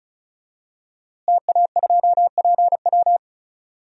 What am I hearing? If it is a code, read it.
Morse code, heard as a keyed tone, about 34 wpm, 700 Hz. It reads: TA2PW
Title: DXCC-100 Call Signs - 5 Characters - Repeated 2x - 35wpm